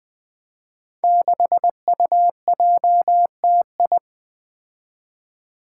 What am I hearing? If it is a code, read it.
Morse code, heard as a keyed tone, about 20 wpm, 700 Hz. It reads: 6UJTI
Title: Five Challenging Letter-Number Combinations 20wpm